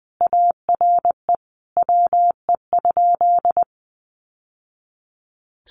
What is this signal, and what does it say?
Morse code, heard as a keyed tone, about 20 wpm, 700 Hz. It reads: ARE WE?
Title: Sentences from Top 1200 Words 20wpm